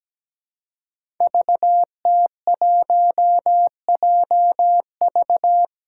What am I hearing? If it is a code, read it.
Morse code, heard as a keyed tone, about 17 wpm, 700 Hz. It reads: VT1JV
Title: Five Challenging Letter-Number Combinations 17wpm